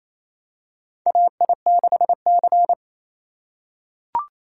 Morse code, heard as a keyed tone, about 28 wpm, 700 Hz. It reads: AI6C
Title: USA Call Signs - 3x Character Spacing - 28wpm